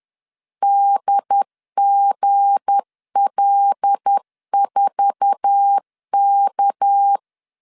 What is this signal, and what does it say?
Morse code, heard as a keyed tone, about 10 wpm, 800 Hz. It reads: DGL4K
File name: LEZIONI / 8 YJ45 / 8 YJ4 10wpm 2633